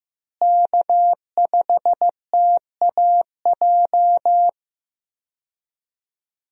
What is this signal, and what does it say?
Morse code, heard as a keyed tone, about 15 wpm, 700 Hz. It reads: K5TAJ
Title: USA Call Signs - 5 Characters - Repeated 3x - 15wpm